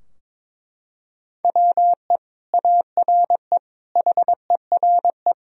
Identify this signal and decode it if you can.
Morse code, heard as a keyed tone, about 22 wpm, 700 Hz. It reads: WE ARE HERE
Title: Sentences from Top 1000 Words - Review Part 4 - 22wpm